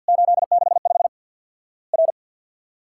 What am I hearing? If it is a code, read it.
Morse code, heard as a keyed tone, about 50 wpm, 700 Hz. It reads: 865 R
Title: Worked All Europe DX Contest - QTC Exchange - 50wpm